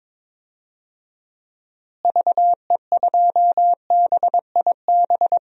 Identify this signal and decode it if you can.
Morse code, heard as a keyed tone, about 22 wpm, 700 Hz. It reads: VE2BIB
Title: DXCC-100 Call Signs - 6 Characters - Repeated 2x - 22wpm